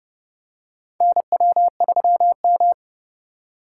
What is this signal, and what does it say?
Morse code, heard as a keyed tone, about 30 wpm, 700 Hz. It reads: NW3M
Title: USA Call Signs - 3x Character Spacing - 30wpm